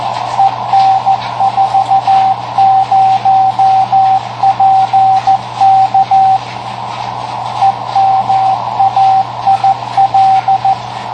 Morse code, heard as a keyed tone, about 14 wpm, 770 Hz, with radio noise. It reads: EN40PK EMAIL
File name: tracks / KA9SZX 3W Macomb IL